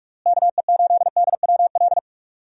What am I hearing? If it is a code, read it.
Morse code, heard as a keyed tone, about 46 wpm, 700 Hz. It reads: KE8DWL
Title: USA Call Signs - 6 Characters - Repeated 3x - 45wpm